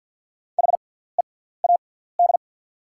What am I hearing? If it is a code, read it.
Morse code, heard as a keyed tone, about 50 wpm, spaced as for 17 wpm, 690 Hz. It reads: HEAD
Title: 6x Character Spacing - Top 300 Words - 50wpm